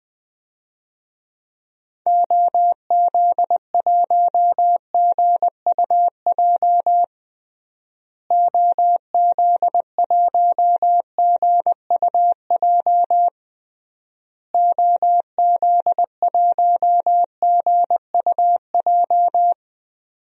Morse code, heard as a keyed tone, about 20 wpm, 700 Hz. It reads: OZ1GUJ OZ1GUJ OZ1GUJ
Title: DXCC-100 Call Signs - 6 Characters - Repeated 3x - 20wpm